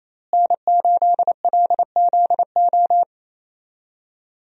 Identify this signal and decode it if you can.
Morse code, heard as a keyed tone, about 28 wpm, 700 Hz. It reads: N8LZO
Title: USA Call Signs - 5 Characters - Repeated 3x - 28wpm